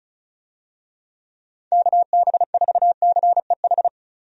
Morse code, heard as a keyed tone, about 34 wpm, 700 Hz. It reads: KB4CEH
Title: USA Call Signs - 6 Characters - 35wpm